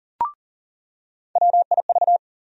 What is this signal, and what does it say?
Morse code, heard as a keyed tone, about 40 wpm, 700 Hz. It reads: WIV
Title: USA Call Sign Postfixes - 3 Letters 40wpm